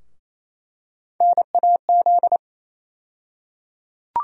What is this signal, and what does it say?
Morse code, heard as a keyed tone, about 28 wpm, 700 Hz. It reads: NAZ
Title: USA Call Sign Postfixes - 3 Letters 28wpm